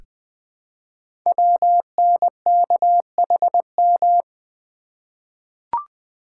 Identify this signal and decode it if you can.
Morse code, heard as a keyed tone, about 20 wpm, 700 Hz. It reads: WNKHM
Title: Code Groups - 5 Letters - 20wpm